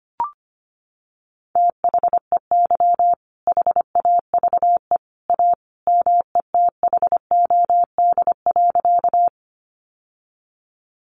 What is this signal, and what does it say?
Morse code, heard as a keyed tone, about 25 wpm, 700 Hz. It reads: THEY HAVE A METHOD.
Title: Sentences from Top 600 Words 25wpm